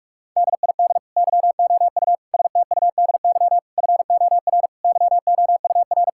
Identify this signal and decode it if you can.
Morse code, heard as a keyed tone, about 42 wpm, 700 Hz. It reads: DID YOU STUDY FOR YOUR
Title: Sentences from Top 1000 Words - Review Part 6 - 45wpm